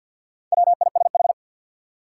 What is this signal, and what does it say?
Morse code, heard as a keyed tone, about 53 wpm, 700 Hz. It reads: WISH
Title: Four Letter Words - 50wpm